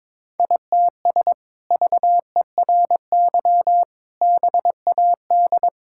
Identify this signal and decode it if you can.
Morse code, heard as a keyed tone, about 21 wpm, 700 Hz. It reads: ITS VERY BAD
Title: Sentences from Top 1000 Words - Review Part 5 - 22wpm